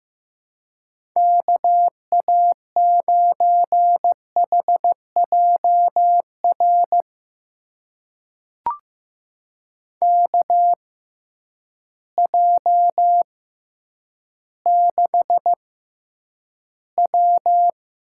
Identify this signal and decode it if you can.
Morse code, heard as a keyed tone, about 15 wpm, spaced as for 9 wpm, 700 Hz. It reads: KA9HJR KJ6W
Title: USA Call Signs - 6 Characters - 6x Character Spacing - 15wpm